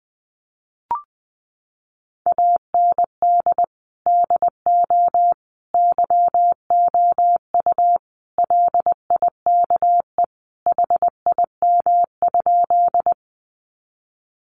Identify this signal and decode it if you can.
Morse code, heard as a keyed tone, about 20 wpm, 700 Hz. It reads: AND DO YOU LIKE HIM?
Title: Sentences from Top 100 Words 20wpm